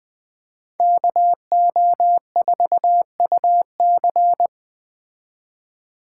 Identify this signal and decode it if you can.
Morse code, heard as a keyed tone, about 20 wpm, 700 Hz. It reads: KO4UC
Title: Speed Racing - USA 2x2 Call Signs 17wpm